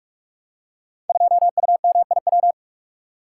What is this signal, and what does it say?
Morse code, heard as a keyed tone, about 44 wpm, 700 Hz. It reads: 2UMIW
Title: Code Groups - 5 Letter:Numbers - 45wpm